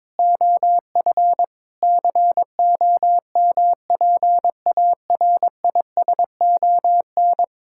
Morse code, heard as a keyed tone, about 22 wpm, 700 Hz. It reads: OF COMPARISON
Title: Sentences from Top 1600 Words 22wpm